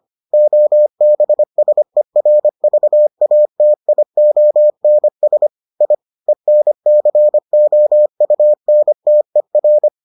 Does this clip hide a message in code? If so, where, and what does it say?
Yes: Morse code from 0.3 to 9.9 s, heard as a keyed tone, about 25 wpm, 600 Hz. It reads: OBSERVATIONS I ENCOUNTER